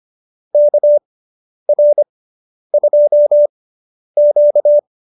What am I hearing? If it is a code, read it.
Morse code, heard as a keyed tone, about 25 wpm, spaced as for 10 wpm, 590 Hz. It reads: KR2Q
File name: QSO220 10